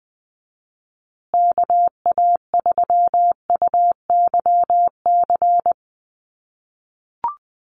Morse code, heard as a keyed tone, about 20 wpm, 700 Hz. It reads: KA3UYC
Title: USA Call Signs Repeated 2x 20wpm